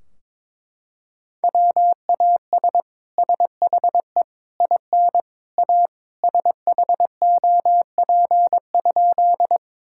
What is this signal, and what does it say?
Morse code, heard as a keyed tone, about 22 wpm, 700 Hz. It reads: WAS SHE IN A SHOP?
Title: Sentences from Top 1600 Words 22wpm